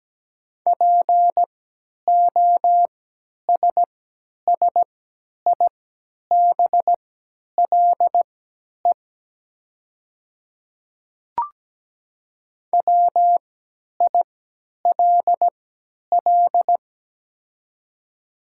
Morse code, heard as a keyed tone, about 17 wpm, spaced as for 10 wpm, 700 Hz. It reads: POSSIBLE WILL
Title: 3x Character Spacing - Top 500 Words - 17wpm